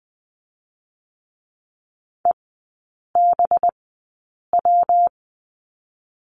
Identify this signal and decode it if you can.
Morse code, heard as a keyed tone, about 20 wpm, spaced as for 8 wpm, 700 Hz. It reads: EBW
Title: 184 Lesson 30e - Letter b - Three Characters 20wpm